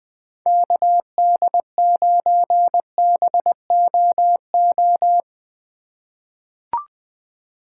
Morse code, heard as a keyed tone, about 20 wpm, 700 Hz. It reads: KD9BOO
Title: USA Call Signs - 2x Character Spacing - 20wpm